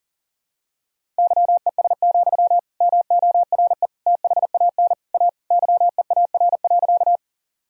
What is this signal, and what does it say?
Morse code, heard as a keyed tone, about 37 wpm, 700 Hz. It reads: YES, MORE THAN A YEAR.
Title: Sentences from Top 1000 Words - Review Part 1 - 40wpm